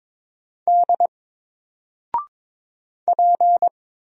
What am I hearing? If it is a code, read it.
Morse code, heard as a keyed tone, about 22 wpm, 700 Hz. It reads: D P
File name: Single-Letter-Number-22wpm